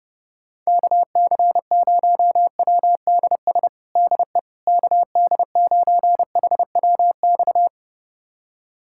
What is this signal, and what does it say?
Morse code, heard as a keyed tone, about 30 wpm, 700 Hz. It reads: KC0WDS DE KD9HWX
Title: USA Callsign DE Callsign - 6 Characters - 30wpm